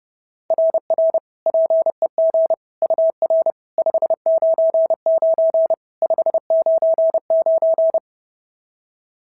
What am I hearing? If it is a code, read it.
Morse code, heard as a keyed tone, about 31 wpm, 690 Hz. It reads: RR PEG UR 599 599
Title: SOTA QSO Practice 30wpm